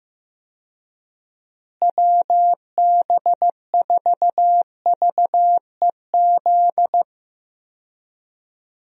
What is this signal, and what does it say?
Morse code, heard as a keyed tone, about 15 wpm, 700 Hz. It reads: WB4VEZ
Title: USA Call Signs - 6 Characters - Repeated 2x - 15wpm